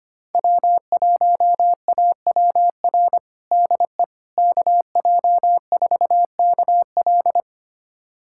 Morse code, heard as a keyed tone, about 25 wpm, 700 Hz. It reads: W1AWR DE KJ4KL
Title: USA Callsign DE Callsign - 5 Characters - 25wpm